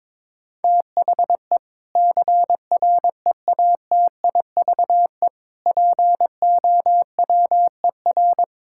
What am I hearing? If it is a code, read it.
Morse code, heard as a keyed tone, about 22 wpm, 700 Hz. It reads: THE CREATIVE POWER
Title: Sentences from Top 1000 Words - Review Part 5 - 22wpm